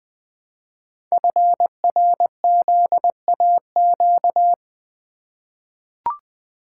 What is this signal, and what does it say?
Morse code, heard as a keyed tone, about 21 wpm, 700 Hz. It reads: FRZAQ
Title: Code Groups - 5 Letters - 20wpm